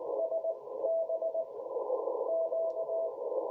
Morse code, heard as a keyed tone, about 18 wpm, spaced as for 14 wpm, 650 Hz, with radio noise. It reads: SB Q
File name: jh8cbh / 2021cw